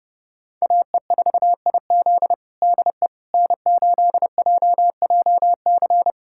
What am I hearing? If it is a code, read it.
Morse code, heard as a keyed tone, about 30 wpm, 700 Hz. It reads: AE4IZ DE N8JJC
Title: USA Callsign DE Callsign - 5 Characters - 30wpm